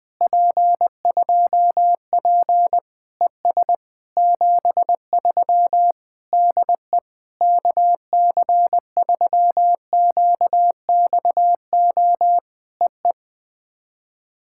Morse code, heard as a keyed tone, about 19 wpm, 700 Hz. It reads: P2P ES 73 DE KC3QXO EE
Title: POTA QSO Practice - 20wpm